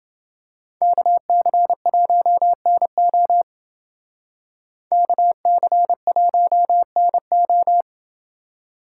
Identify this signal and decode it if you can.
Morse code, heard as a keyed tone, about 28 wpm, 700 Hz. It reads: KC1NO KC1NO